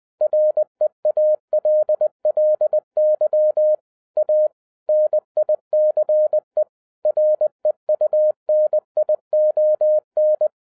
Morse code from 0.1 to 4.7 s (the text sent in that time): REALLY A